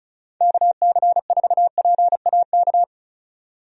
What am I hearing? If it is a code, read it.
Morse code, heard as a keyed tone, about 36 wpm, 700 Hz. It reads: KC4PAK
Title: USA Call Signs - 6 Characters - 35wpm